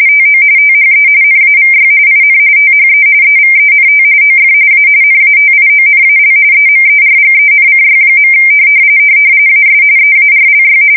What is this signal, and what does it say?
RTTY, about 45.45 baud. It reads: TO THE NAVY IN THE EVENT OF A NATIONAL EMERGENCY. AS EARLY
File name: fsk170 / battleshipnj